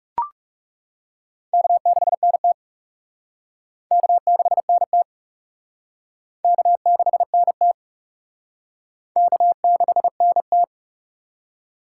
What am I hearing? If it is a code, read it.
Morse code, heard as a keyed tone, about 35 wpm, 700 Hz. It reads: K6NT K6NT K6NT K6NT
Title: Speed Racing - USA 1x2 Call Signs 30wpm